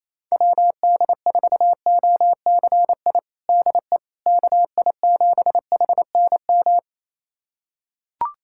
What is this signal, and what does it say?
Morse code, heard as a keyed tone, about 28 wpm, 700 Hz. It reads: WD4OCI DE KI7HNM
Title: USA Callsign DE Callsign - 6 Characters - 28wpm